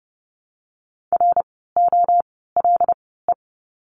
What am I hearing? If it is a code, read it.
Morse code, heard as a keyed tone, about 30 wpm, 700 Hz. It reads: R O L E
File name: Lesson 16f - Letter c - Words 30wpm